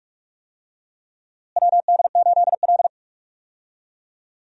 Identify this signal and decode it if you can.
Morse code, heard as a keyed tone, about 46 wpm, 700 Hz. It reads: WD8L
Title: USA Call Signs - 4 Characters - Repeated 2x - 45wpm